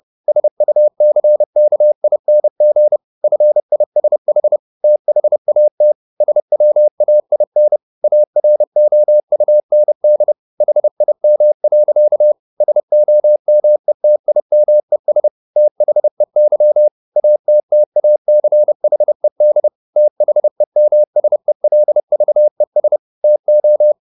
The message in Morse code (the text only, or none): SUCKING FISH THAT SWAIN AROUND HIM. SOMETIMES THEY ATTACHED THEMSELVES TO